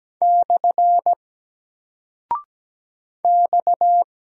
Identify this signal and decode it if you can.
Morse code, heard as a keyed tone, about 17 wpm, 700 Hz. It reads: / X
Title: Single Character (Letter:Number:Punctuation:Prosign) 17wpm